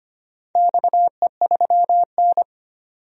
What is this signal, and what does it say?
Morse code, heard as a keyed tone, about 25 wpm, 700 Hz. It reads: XE3N
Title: DXCC-100 Call Signs - 4 Characters - 25wpm